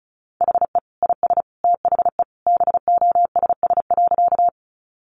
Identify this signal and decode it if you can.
Morse code, heard as a keyed tone, about 34 wpm, 700 Hz. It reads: HE IS THE BOSS.
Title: Sentences from Top 700 Words 35wpm